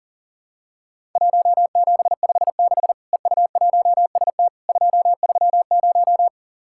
Morse code, heard as a keyed tone, about 43 wpm, 700 Hz. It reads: 1756 EU1ST 230
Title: Worked All Europe DX Contest - QTC Exchange - 40wpm